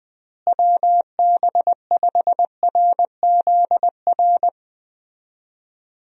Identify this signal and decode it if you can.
Morse code, heard as a keyed tone, about 20 wpm, 700 Hz. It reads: WB5RZR
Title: USA Call Signs - 6 Characters - Repeated 3x - 20wpm